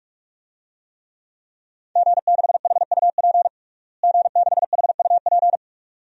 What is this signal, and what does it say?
Morse code, heard as a keyed tone, about 42 wpm, 700 Hz. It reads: G6HUP G6HUP
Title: DXCC-100 Call Signs - 5 Characters - Repeated 2x - 45wpm